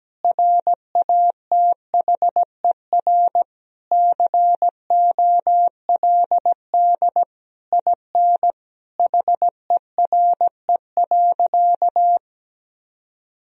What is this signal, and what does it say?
Morse code, heard as a keyed tone, about 17 wpm, 700 Hz. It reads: RATHER COLD IN HERE.
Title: Sentences from Top 1000 Words - Review Part 4 - 17wpm